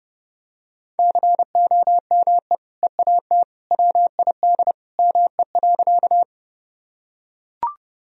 Morse code, heard as a keyed tone, about 30 wpm, 700 Hz. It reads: COME EAT WID ME.
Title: Sentences from Top 300 Words and Abbreviations 30wpm